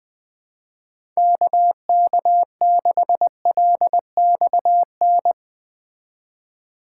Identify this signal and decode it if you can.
Morse code, heard as a keyed tone, about 19 wpm, 700 Hz. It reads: KK6LXN
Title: Speed Racing - USA 2x3 Call Signs 17wpm